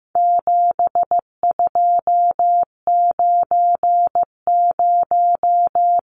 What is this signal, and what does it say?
Morse code, heard as a keyed tone, about 15 wpm, 700 Hz. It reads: 7290
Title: Numbers - 4-Digit 15wpm